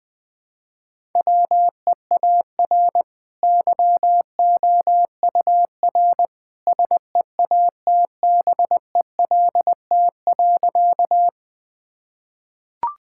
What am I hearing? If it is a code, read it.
Morse code, heard as a keyed tone, about 20 wpm, 700 Hz. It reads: WEAR YOUR SEATBELT.
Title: Easy Sentences with Hints - 20wpm